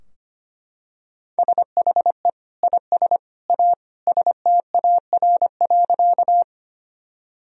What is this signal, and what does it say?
Morse code, heard as a keyed tone, about 25 wpm, 700 Hz. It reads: SHE IS A STAR.
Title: Sentences from Top 1000 Words - Review Part 2 - 25wpm